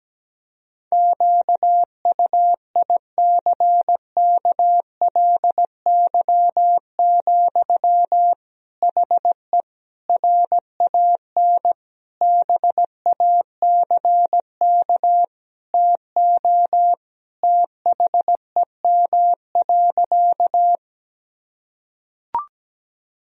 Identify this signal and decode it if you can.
Morse code, heard as a keyed tone, about 17 wpm, 700 Hz. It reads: QUICKLY, HE RAN BACK TO THEM.
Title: Sentences from Top 1000 Words - Review Part 3 - 17wpm